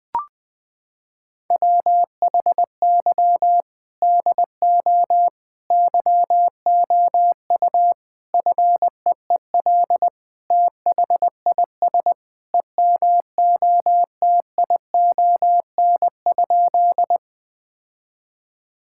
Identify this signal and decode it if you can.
Morse code, heard as a keyed tone, about 20 wpm, 700 Hz. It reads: WHY DO YOU FEEL THIS EMOTION?
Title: Sentences from Top 1600 Words 20wpm